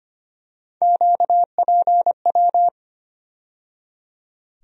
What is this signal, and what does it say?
Morse code, heard as a keyed tone, about 25 wpm, 700 Hz. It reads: QPW
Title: USA Call Sign Postfixes - 3 Letters 25wpm